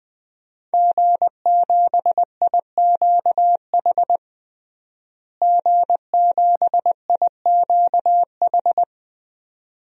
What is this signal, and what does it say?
Morse code, heard as a keyed tone, about 20 wpm, 700 Hz. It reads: G7IQH G7IQH
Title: DXCC-100 Call Signs - 5 Characters - Repeated 2x - 20wpm